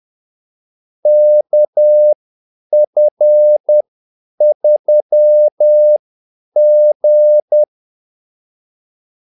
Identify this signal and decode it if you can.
Morse code, heard as a keyed tone, about 10 wpm, spaced as for 8 wpm, 600 Hz. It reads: KF3G
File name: Callsigns-10-8wpm